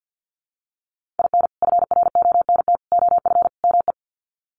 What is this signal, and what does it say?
Morse code, heard as a keyed tone, about 50 wpm, 700 Hz. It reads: IN FRONT OF ME
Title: Sets of 4 Words 50wpm